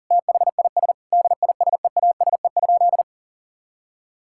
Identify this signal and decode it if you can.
Morse code, heard as a keyed tone, about 40 wpm, 700 Hz. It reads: THIS DISEASE?
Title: Sentences from Top 1000 Words - Review Part 6 - 40wpm